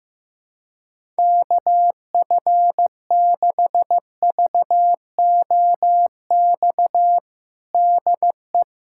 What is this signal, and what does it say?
Morse code, heard as a keyed tone, about 15 wpm, 700 Hz. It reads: KF6VOX DE
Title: USA Callsign DE Callsign - 6 Characters - 15wpm